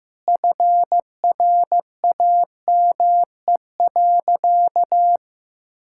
Morse code, heard as a keyed tone, about 15 wpm, 700 Hz. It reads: FRAME.
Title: Sentences from Top 1000 Words - Review Part 6 - 15wpm